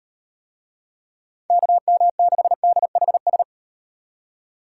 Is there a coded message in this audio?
Yes: Morse code from 1.5 to 3.4 s, heard as a keyed tone, about 35 wpm, 700 Hz. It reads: KM6DHS